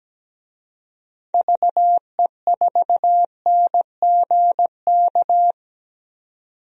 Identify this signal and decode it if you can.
Morse code, heard as a keyed tone, about 17 wpm, 700 Hz. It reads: VE4NGK
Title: DXCC-100 Call Signs - 6 Characters - 17wpm